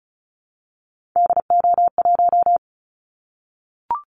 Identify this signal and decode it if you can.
Morse code, heard as a keyed tone, about 36 wpm, 690 Hz. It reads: DO1
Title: Callsign Prefixes for Top 50 DXCC - 35wpm